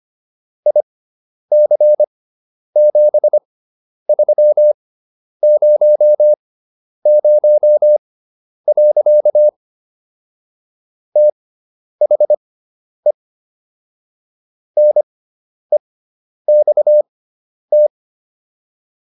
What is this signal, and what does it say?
Morse code, heard as a keyed tone, about 25 wpm, spaced as for 10 wpm, 600 Hz. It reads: IC7300. THE NEXT